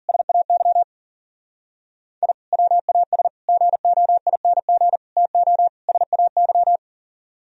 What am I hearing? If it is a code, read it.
Morse code, heard as a keyed tone, about 53 wpm, 700 Hz. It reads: SAY I WAS GOING TO SAY